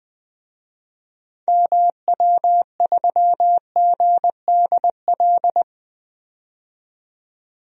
Morse code, heard as a keyed tone, about 21 wpm, 700 Hz. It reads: MW3GDL
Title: DXCC-100 Call Signs - 6 Characters - 20wpm